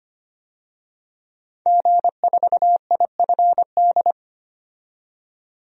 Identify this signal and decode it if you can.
Morse code, heard as a keyed tone, about 25 wpm, 700 Hz. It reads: G4IFD
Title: DXCC-100 Call Signs - 5 Characters - 25wpm